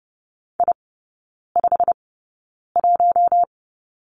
Morse code, heard as a keyed tone, about 30 wpm, spaced as for 9 wpm, 700 Hz. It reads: I51
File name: course / 30:10wpm / 087 Lesson 15e - Number 5 - Three Characters 30wpm